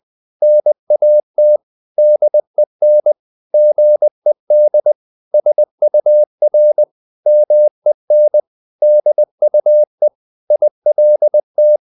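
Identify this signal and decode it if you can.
Morse code, heard as a keyed tone, about 20 wpm, 600 Hz. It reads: NAT DEN GED SUR MEN DUE ILT
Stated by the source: Morse code practice of short Danish words at 20 wpm, from CW med Gnister.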